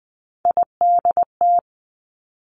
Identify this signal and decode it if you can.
Morse code, heard as a keyed tone, about 20 wpm, 700 Hz. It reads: IDT
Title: Three Challenging Letter-Number Combinations 20wpm